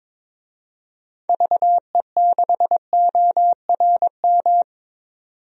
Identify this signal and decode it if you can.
Morse code, heard as a keyed tone, about 22 wpm, 700 Hz. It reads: VE6ORM
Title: DXCC-100 Call Signs - 6 Characters - 22wpm